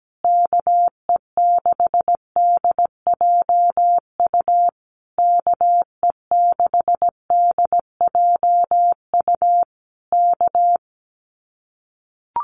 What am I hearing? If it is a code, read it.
Morse code, heard as a keyed tone, about 17 wpm, 700 Hz. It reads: KE6DJU KE6DJU K
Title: Common QSO Phrases Encore 17wpm